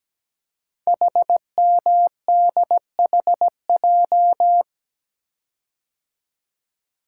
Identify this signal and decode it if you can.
Morse code, heard as a keyed tone, about 17 wpm, 700 Hz. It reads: HMDHJ
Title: Code Groups - 5 Letters - 17wpm